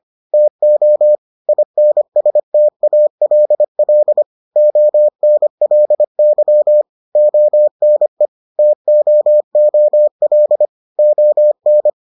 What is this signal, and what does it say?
Morse code, heard as a keyed tone, about 25 wpm, 600 Hz. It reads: TO INSTALL ONLY ONE TOOL ON